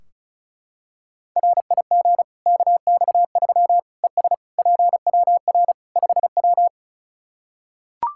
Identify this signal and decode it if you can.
Morse code, heard as a keyed tone, about 34 wpm, 700 Hz. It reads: RIG KX3 ES PWR 5W
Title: Practice QSO - Exchange 2 - Info - 35wpm